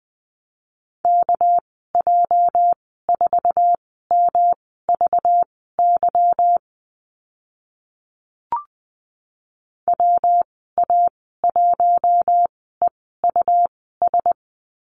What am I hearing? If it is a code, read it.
Morse code, heard as a keyed tone, about 20 wpm, spaced as for 15 wpm, 700 Hz. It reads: KJ4MVY WA1EUS
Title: 238 Lesson 38g - Letter q - Call Signs 20wpm